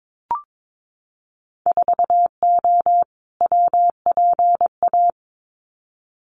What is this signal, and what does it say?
Morse code, heard as a keyed tone, about 22 wpm, 700 Hz. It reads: 4O WPA
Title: Winter Field Day Exchanges 22wpm